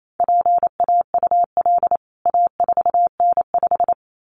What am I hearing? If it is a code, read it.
Morse code, heard as a keyed tone, about 28 wpm, 700 Hz. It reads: PAUL A4N5
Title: CWT Exchange 28wpm